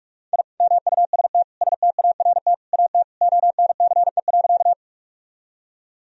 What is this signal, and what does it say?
Morse code, heard as a keyed tone, about 46 wpm, 700 Hz. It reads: I MUST START AT ONCE.
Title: Sentences from Top 1000 Words - Review Part 1 - 45wpm